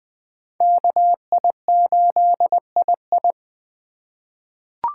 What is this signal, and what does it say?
Morse code, heard as a keyed tone, about 20 wpm, 700 Hz. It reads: KI8II
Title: USA Call Signs - 5 Characters - 3x Character Spacing - 20wpm